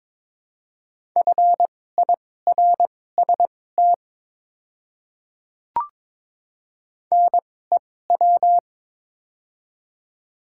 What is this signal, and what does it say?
Morse code, heard as a keyed tone, about 22 wpm, spaced as for 16 wpm, 690 Hz. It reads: FIRST NEW
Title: 2x Character Spacing - Top 500 Words - 22wpm